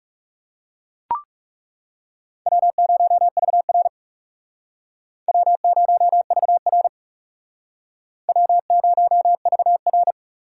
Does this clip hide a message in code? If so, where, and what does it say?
Yes: Morse code from 2.5 to 10.1 s, heard as a keyed tone, about 38 wpm, 700 Hz. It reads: W0VR W0VR W0VR